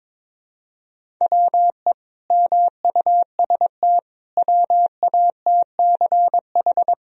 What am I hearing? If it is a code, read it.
Morse code, heard as a keyed tone, about 22 wpm, 700 Hz. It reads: WE MUST WATCH
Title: Sentences from Top 1000 Words - Review Part 1 - 22wpm